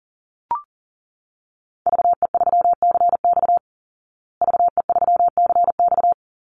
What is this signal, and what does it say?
Morse code, heard as a keyed tone, about 40 wpm, 700 Hz. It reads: VE3CX VE3CX
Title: Call Signs Repeated 2x - 40wpm